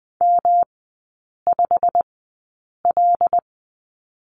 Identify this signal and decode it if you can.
Morse code, heard as a keyed tone, about 20 wpm, spaced as for 8 wpm, 690 Hz. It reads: M5L
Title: 205 Lesson 33e - Letter j - Three Characters 20wpm